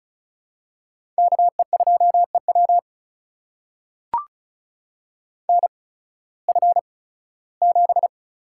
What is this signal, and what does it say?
Morse code, heard as a keyed tone, about 36 wpm, spaced as for 19 wpm, 700 Hz. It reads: KE2EW NF7
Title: USA Call Signs - 8x Character Spacing - 35wpm